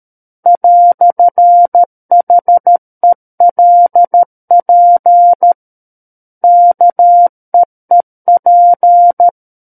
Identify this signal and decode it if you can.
Morse code, heard as a keyed tone, about 13 wpm, 700 Hz. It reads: "HELP KEEP
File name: Book One / 04-Chapter 4 - 13 WPM